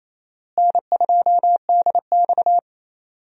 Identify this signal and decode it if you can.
Morse code, heard as a keyed tone, about 28 wpm, 700 Hz. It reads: N2DX
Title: USA Call Signs - 4 Characters - Repeated 3x - 28wpm